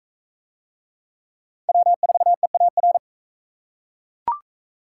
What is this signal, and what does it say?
Morse code, heard as a keyed tone, about 40 wpm, 700 Hz. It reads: W4EAR